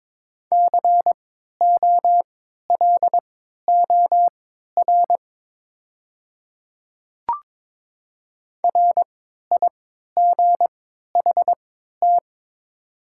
Morse code, heard as a keyed tone, about 22 wpm, spaced as for 13 wpm, 700 Hz. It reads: COLOR RIGHT
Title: 3x Character Spacing - Top 300 Words - 22wpm